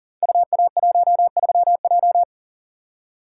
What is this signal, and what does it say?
Morse code, heard as a keyed tone, about 41 wpm, 700 Hz. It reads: UA13J
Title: Code Groups - 5 Letter:Numbers - 40wpm